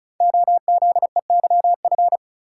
Morse code, heard as a keyed tone, about 36 wpm, 700 Hz. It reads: OZEYF
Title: Code Groups - 5 Letter:Numbers - 35wpm